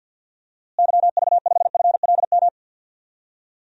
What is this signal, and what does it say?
Morse code, heard as a keyed tone, about 52 wpm, 700 Hz. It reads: YV5FLM
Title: DXCC-100 Call Signs - 6 Characters - Repeated 3x - 50wpm